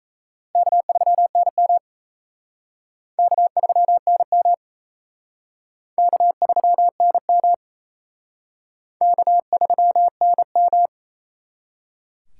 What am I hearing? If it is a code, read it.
Morse code, heard as a keyed tone, about 37 wpm, 700 Hz. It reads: K3NM K3NM K3NM K3NM